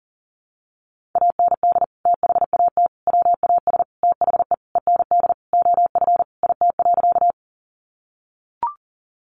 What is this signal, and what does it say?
Morse code, heard as a keyed tone, about 40 wpm, 700 Hz. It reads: AND THAT WAS THE END OF IT.
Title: Sentences from Top 200 Words 40wpm